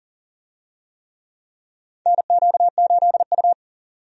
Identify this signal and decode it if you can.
Morse code, heard as a keyed tone, about 40 wpm, 700 Hz. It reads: NQ8U
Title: USA Call Signs - 4 Characters - 40wpm